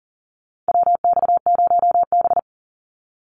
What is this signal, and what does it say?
Morse code, heard as a keyed tone, about 40 wpm, 700 Hz. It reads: WX0B
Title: Call Signs - 40wpm